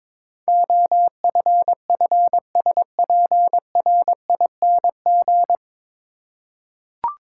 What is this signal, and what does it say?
Morse code, heard as a keyed tone, about 22 wpm, 700 Hz. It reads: OFFSPRING
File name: 9 Letter Words - First 3 Letters Given - 22wpm